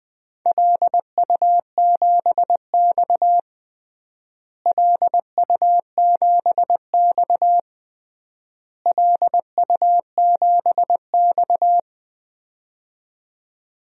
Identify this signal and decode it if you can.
Morse code, heard as a keyed tone, about 20 wpm, 700 Hz. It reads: LU7X LU7X LU7X
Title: DXCC-100 Call Signs - 4 Characters - Repeated 3x - 20wpm